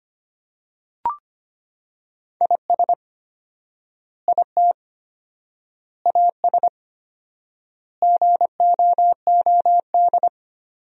Morse code, heard as a keyed tone, about 25 wpm, 700 Hz. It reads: IS IT AS GOOD
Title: 4x Word-Spacing - Sentences from Top 200 Words - 25wpm